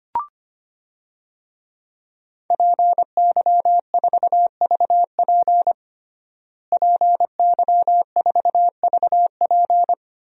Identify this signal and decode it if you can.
Morse code, heard as a keyed tone, about 25 wpm, 700 Hz. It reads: PY4VP PY4VP
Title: DXCC-100 Call Signs - 5 Characters - Repeated 3x - 25wpm